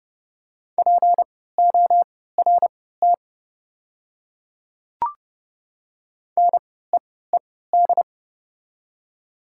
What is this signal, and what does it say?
Morse code, heard as a keyed tone, about 31 wpm, spaced as for 17 wpm, 690 Hz. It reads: PORT NEED
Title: 3x Character Spacing - Top 200 Words - 30wpm